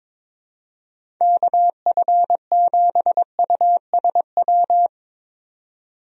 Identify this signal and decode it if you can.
Morse code, heard as a keyed tone, about 22 wpm, 700 Hz. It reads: KF7USW
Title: USA Call Signs - 6 Characters - 22wpm